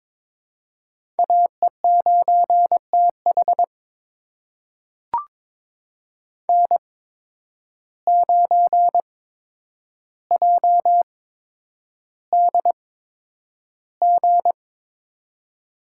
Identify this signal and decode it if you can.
Morse code, heard as a keyed tone, about 23 wpm, spaced as for 10 wpm, 700 Hz. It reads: AE9TH N9JDG
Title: USA Call Signs - 5 Characters - 8x Character Spacing - 22wpm